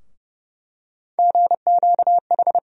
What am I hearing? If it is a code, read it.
Morse code, heard as a keyed tone, about 32 wpm, 700 Hz. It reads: GQH
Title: USA Call Sign Postfixes - 3 Letters 30wpm